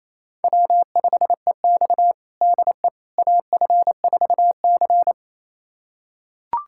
Morse code, heard as a keyed tone, about 28 wpm, 700 Hz. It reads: W5EX DE AF4C
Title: USA Callsign DE Callsign - 4 Characters - 28wpm